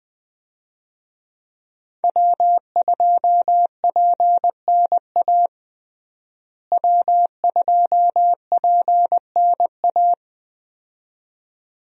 Morse code, heard as a keyed tone, about 20 wpm, 700 Hz. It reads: W2PNA W2PNA
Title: USA Call Signs - 5 Characters - Repeated 2x - 20wpm